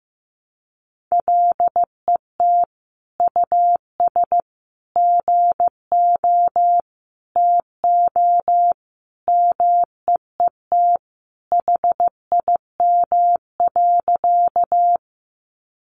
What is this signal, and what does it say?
Morse code, heard as a keyed tone, about 15 wpm, 700 Hz. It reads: LET US GO TO MEET HIM.
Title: Sentences from Top 1100 Words 15wpm